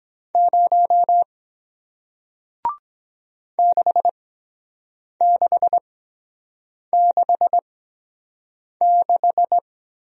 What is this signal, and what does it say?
Morse code, heard as a keyed tone, about 23 wpm, spaced as for 7 wpm, 700 Hz. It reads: 0 6666